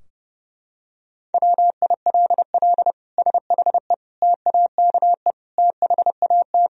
Morse code, heard as a keyed tone, about 32 wpm, 700 Hz. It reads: WILL SHE TAKE THAT
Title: Sentences from Top 1000 Words - Review Part 4 - 30wpm